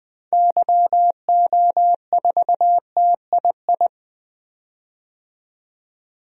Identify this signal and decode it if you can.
Morse code, heard as a keyed tone, about 20 wpm, 700 Hz. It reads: YO4TII
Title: DXCC-100 Call Signs - 6 Characters - 20wpm